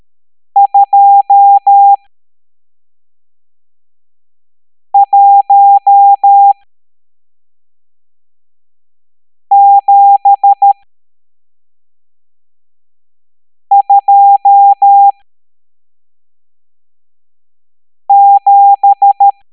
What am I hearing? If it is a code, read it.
Morse code, heard as a keyed tone, about 13 wpm, spaced as for 3 wpm, 800 Hz. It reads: 21727
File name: hamdata / L12-EX02-CHIFFRES